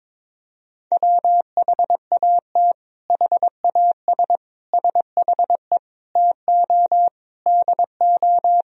Morse code, heard as a keyed tone, about 22 wpm, 700 Hz. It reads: WHAT HAS SHE TO DO
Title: Sentences from Top 100 Words and Abbreviations 22wpm